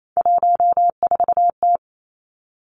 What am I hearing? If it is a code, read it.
Morse code, heard as a keyed tone, about 28 wpm, 700 Hz. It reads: 14T
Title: Three Challenging Letter-Number Combinations 28wpm